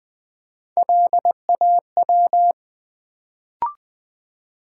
Morse code, heard as a keyed tone, about 20 wpm, 700 Hz. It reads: LAW
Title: Build A Word - 1 Letter At A Time - 3-Letter Words - 20wpm